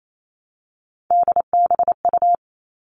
Morse code, heard as a keyed tone, about 28 wpm, 700 Hz. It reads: DBU
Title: All Cut Numbers - 3-Digit 28wpm